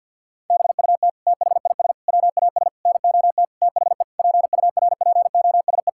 Morse code, heard as a keyed tone, about 50 wpm, 700 Hz. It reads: BUT THIS WAS NOT THE PURPOSE
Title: Sentences from Top 1000 Words - Review Part 6 - 50wpm